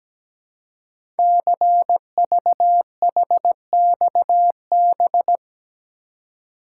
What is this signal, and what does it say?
Morse code, heard as a keyed tone, about 17 wpm, 700 Hz. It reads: CVHXB
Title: Code Groups - 5 Letters - 17wpm